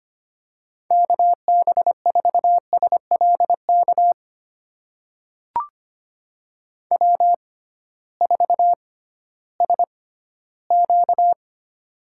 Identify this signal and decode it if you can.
Morse code, heard as a keyed tone, about 25 wpm, spaced as for 14 wpm, 700 Hz. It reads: KB4SLK W4SQ
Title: USA Call Signs - 6x Character Spacing - 25wpm